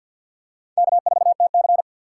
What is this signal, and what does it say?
Morse code, heard as a keyed tone, about 46 wpm, 700 Hz. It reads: K4TC
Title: USA Call Signs - 3x Character Spacing - 50wpm